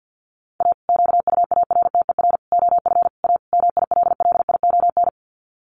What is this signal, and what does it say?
Morse code, heard as a keyed tone, about 50 wpm, 700 Hz. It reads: A QUARTER OF A MILLION
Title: Sets of 5 Words Encore2 50wpm